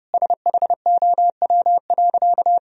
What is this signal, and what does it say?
Morse code, heard as a keyed tone, about 32 wpm, 700 Hz. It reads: SHOW.
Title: Sentences from Top 1000 Words - Review Part 2 - 30wpm